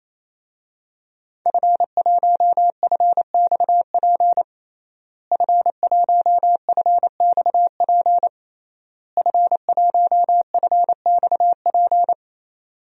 Morse code, heard as a keyed tone, about 28 wpm, 690 Hz. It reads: F1FXP F1FXP F1FXP
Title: DXCC-100 Call Signs - 5 Characters - Repeated 3x - 28wpm